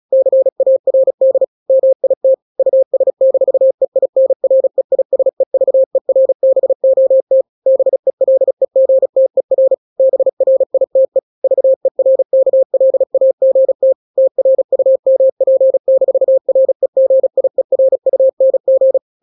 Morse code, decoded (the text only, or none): CARD MIT US-EINREISEVERBOT BELEGTER BRITE VERKLAGT TRUMP-REGIERUNG